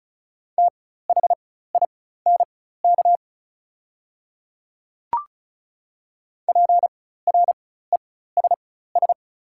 Morse code, heard as a keyed tone, about 36 wpm, spaced as for 16 wpm, 690 Hz. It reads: THINK PRESS
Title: 4x Character Spacing - Top 500 Words - 35wpm